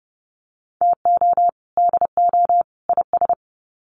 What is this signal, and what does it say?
Morse code, heard as a keyed tone, about 30 wpm, 700 Hz. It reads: TO DO IS
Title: Sets of 3 Words 30wpm